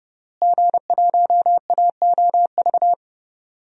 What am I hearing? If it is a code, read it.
Morse code, heard as a keyed tone, about 30 wpm, 700 Hz. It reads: G1AOV
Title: DXCC-100 Call Signs - 5 Characters - Repeated 3x - 30wpm